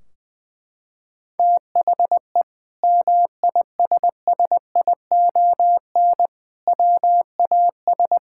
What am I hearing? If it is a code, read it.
Morse code, heard as a keyed tone, about 21 wpm, 700 Hz. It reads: THE MISSION WAS
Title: Sentences from Top 1600 Words 20wpm